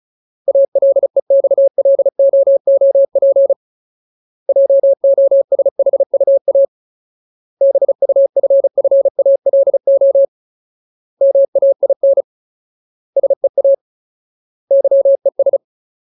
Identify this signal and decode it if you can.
Morse code, heard as a keyed tone, about 34 wpm, 550 Hz. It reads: ALEXLOOP JOSHUA BUFFALO MAIN SEA YES
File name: Ultimate Word List / 35WPM Ultimate